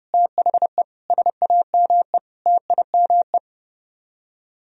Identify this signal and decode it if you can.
Morse code, heard as a keyed tone, about 38 wpm, 700 Hz. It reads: THE SAME TIME